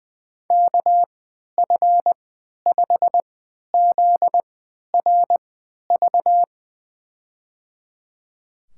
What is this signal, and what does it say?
Morse code, heard as a keyed tone, about 20 wpm, spaced as for 11 wpm, 700 Hz. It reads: KF5ZRV
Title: USA Call Signs - 3x Character Spacing - 20wpm